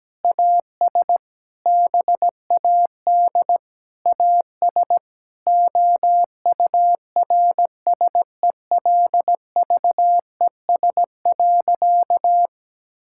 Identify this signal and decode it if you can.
Morse code, heard as a keyed tone, about 17 wpm, 700 Hz. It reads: AS BAD AS OURSELVES.
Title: Sentences from Top 1600 Words 17wpm